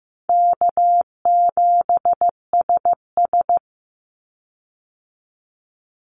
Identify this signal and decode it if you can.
Morse code, heard as a keyed tone, about 15 wpm, 700 Hz. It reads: K7SS
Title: Call Signs Repeated 2x - 15wpm